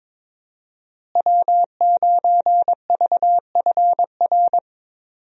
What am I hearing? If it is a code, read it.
Morse code, heard as a keyed tone, about 22 wpm, 700 Hz. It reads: W9VFR
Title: USA Call Signs - 5 Characters - Repeated 3x - 22wpm